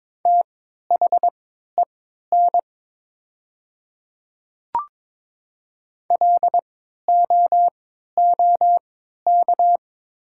Morse code, heard as a keyed tone, about 22 wpm, spaced as for 13 wpm, 700 Hz. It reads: THEN LOOK
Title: 3x Character Spacing - Top 100 Words - 22wpm